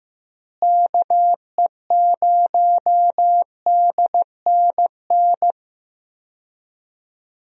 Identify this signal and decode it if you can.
Morse code, heard as a keyed tone, about 15 wpm, 700 Hz. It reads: KE0DNN
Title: USA Call Signs - 6 Characters - Repeated 2x - 15wpm